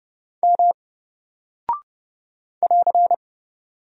Morse code, heard as a keyed tone, about 30 wpm, 700 Hz. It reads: M <AR>
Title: Single Character (Letter:Number:Punctuation:Prosign) 30wpm